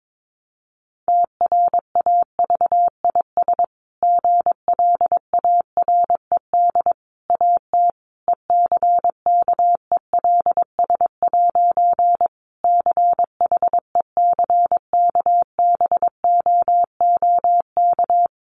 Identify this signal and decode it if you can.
Morse code, heard as a keyed tone, about 22 wpm, 700 Hz. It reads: TRAVIS GLARED AT ECKELS' CHECKBOOK
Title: A Sound Of Thunder 22wpm